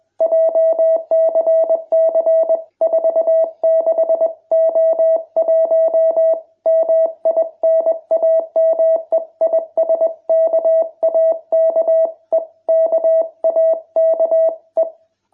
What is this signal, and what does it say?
Morse code, heard as a keyed tone, about 21 wpm, 650 Hz. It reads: JCC46O1 MI NAME IS KAKE KAKE